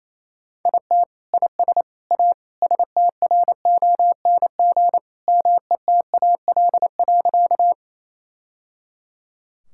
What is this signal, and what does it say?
Morse code, heard as a keyed tone, about 28 wpm, 700 Hz. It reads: IT IS A STRONG METAL.
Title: Sentences from Top 1000 Words - Review Part 6 - 28wpm